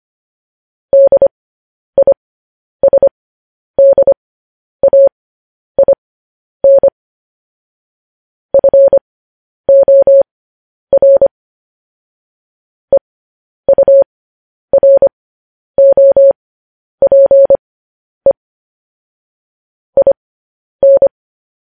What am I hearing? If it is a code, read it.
Morse code, heard as a keyed tone, about 25 wpm, spaced as for 10 wpm, 550 Hz. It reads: DISDAIN FOR EUROPE IN
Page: news-headlines-in-morse-code-at-10-wpm